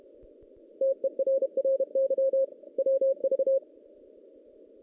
Morse code, heard as a keyed tone, about 32 wpm, 540 Hz. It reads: TERRY WV
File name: 2016 JAN CW NAQP / 2016 Jan naqp k5ot - 0-5